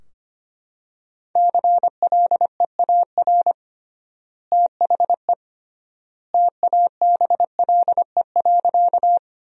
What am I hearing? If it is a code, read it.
Morse code, heard as a keyed tone, about 25 wpm, 700 Hz. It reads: CLEAR THE TABLE.
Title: 3x Word-Spacing - Sentences from Top 500 Words - 25wpm